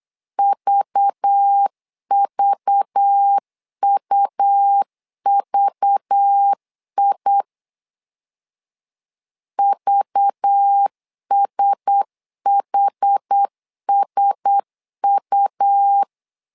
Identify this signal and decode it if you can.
Morse code, heard as a keyed tone, about 8 wpm, 800 Hz. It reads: VVUVI VSHSU
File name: LEZIONI / 5 LBVC / 5 LBV 8wpm 2837